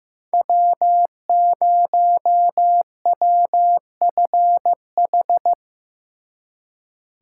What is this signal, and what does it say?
Morse code, heard as a keyed tone, about 17 wpm, 700 Hz. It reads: W0WFH
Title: Speed Racing - USA Call Signs 15wpm